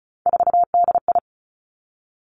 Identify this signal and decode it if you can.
Morse code, heard as a keyed tone, about 36 wpm, 700 Hz. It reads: 4DI
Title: Three Challenging Letter-Number Combinations 35wpm